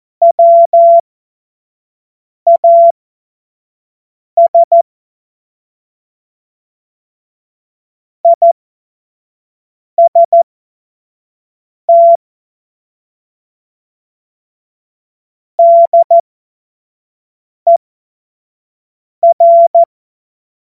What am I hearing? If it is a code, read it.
Morse code, heard as a keyed tone, about 14 wpm, spaced as for 5 wpm, 680 Hz. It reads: WAS IST DER
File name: Morsekurs / Witze / Witz 14.November 2023 25 F70